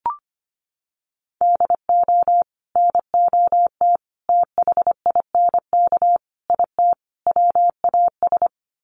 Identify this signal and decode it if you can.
Morse code, heard as a keyed tone, about 25 wpm, 700 Hz. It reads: DO NOT THINK IT WAS
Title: Sets of 5 Words Encore 25wpm